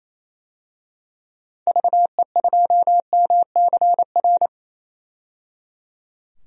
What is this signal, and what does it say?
Morse code, heard as a keyed tone, about 27 wpm, 700 Hz. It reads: VE2MCR
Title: DXCC-100 Call Signs - 6 Characters - 28wpm